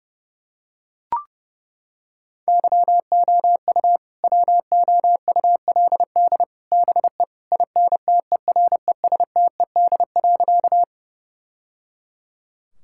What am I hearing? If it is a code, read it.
Morse code, heard as a keyed tone, about 30 wpm, 700 Hz. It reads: YOU WOULD BE INTERESTED.
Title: Sentences from Top 1000 Words - Review Part 6 - 30wpm